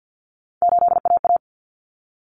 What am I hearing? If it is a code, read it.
Morse code, heard as a keyed tone, about 50 wpm, 700 Hz. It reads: 8AA
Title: Common Cut Numbers - 3-Digit 50wpm